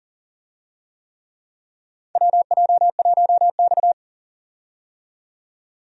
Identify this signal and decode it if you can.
Morse code, heard as a keyed tone, about 40 wpm, 700 Hz. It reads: WJ1X
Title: USA Call Signs - 4 Characters - 40wpm